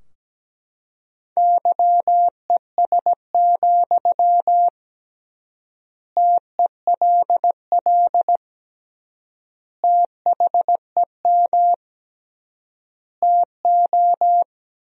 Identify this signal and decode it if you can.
Morse code, heard as a keyed tone, about 17 wpm, 700 Hz. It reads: YES, TELL THEM TO
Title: 3x Word-Spacing - Sentences from Top 500 Words - 17wpm